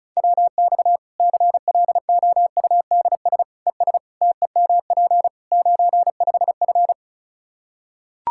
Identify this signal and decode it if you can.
Morse code, heard as a keyed tone, about 34 wpm, 700 Hz. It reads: WX CLOUDS ES TEMP 95F
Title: Practice QSO - Exchange 2 - Info - 35wpm